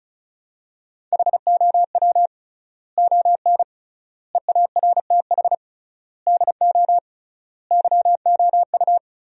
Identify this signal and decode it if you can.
Morse code, heard as a keyed tone, about 36 wpm, 700 Hz. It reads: HOW ON EARTH DO YOU
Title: 3x Word-Spacing - Sentences from Top 200 Words - 35wpm